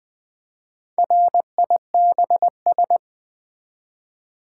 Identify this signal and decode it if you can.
Morse code, heard as a keyed tone, about 20 wpm, 700 Hz. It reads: RIBS
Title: Easy Words with Hints - 20wpm